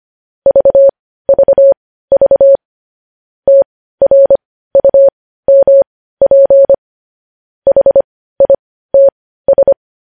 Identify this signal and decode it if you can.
Morse code, heard as a keyed tone, about 25 wpm, spaced as for 15 wpm, 550 Hz. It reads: VVV TRUMP HITS
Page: news headlines in morse code at 15 wpm